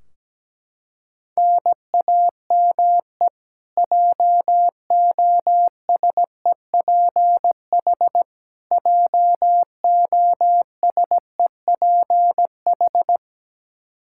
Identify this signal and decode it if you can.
Morse code, heard as a keyed tone, about 17 wpm, 700 Hz. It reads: NAME JOSEPH JOSEPH
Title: Practice QSO - Exchange 1 - Beginning - 17wpm